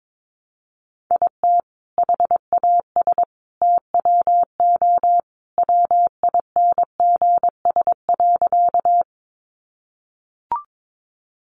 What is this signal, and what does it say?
Morse code, heard as a keyed tone, about 22 wpm, 700 Hz. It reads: IT HAS TWO WINGS.
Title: Additional Easy Phrases 22wpm